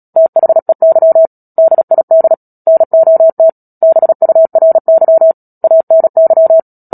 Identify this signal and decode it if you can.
Morse code, heard as a keyed tone, about 37 wpm, 650 Hz. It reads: THEY DID NOT BURY ANY
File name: Book Two / 10-Chapter 10 - Epilogue - 36 WPM